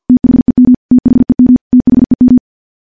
Morse code, heard as a keyed tone, about 50 wpm, 250 Hz. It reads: THEM THEM THEM
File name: morse / morse-training-50WPM